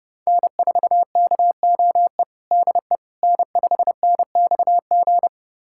Morse code, heard as a keyed tone, about 30 wpm, 700 Hz. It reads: N4KOE DE N5NXG
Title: USA Callsign DE Callsign - 5 Characters - 30wpm